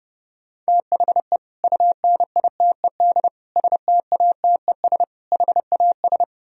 Morse code, heard as a keyed tone, about 36 wpm, 700 Hz. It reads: THE UNITED STATES HAS
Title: Speed-Racing - Sets of 4 Words 30wpm